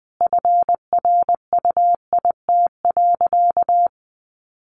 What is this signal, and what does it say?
Morse code, heard as a keyed tone, about 20 wpm, 700 Hz. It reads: FRUIT.
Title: Sentences from Top 1300 Words 20wpm